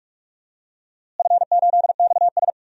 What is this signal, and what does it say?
Morse code, heard as a keyed tone, about 46 wpm, 700 Hz. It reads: F8XS
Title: DXCC-100 Call Signs - 4 Characters - Repeated 3x - 45wpm